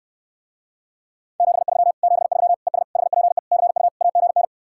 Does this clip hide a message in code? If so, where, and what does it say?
Yes: Morse code from 1.4 to 4.4 s, heard as a keyed tone, about 67 wpm, 700 Hz. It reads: 73 73 ES HPE CU AGN